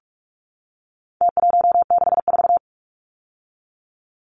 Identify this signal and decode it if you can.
Morse code, heard as a keyed tone, about 44 wpm, 700 Hz. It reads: T164
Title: Numbers - Contest Style - 4-Digit 45wpm